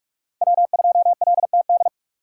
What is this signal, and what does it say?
Morse code, heard as a keyed tone, about 43 wpm, 700 Hz. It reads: W2LTD
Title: USA Call Signs - 5 Characters - 8x Character Spacing - 45wpm